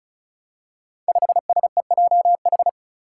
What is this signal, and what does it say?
Morse code, heard as a keyed tone, about 36 wpm, 700 Hz. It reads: 5SEJH
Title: Five Challenging Letter-Number Combinations 35wpm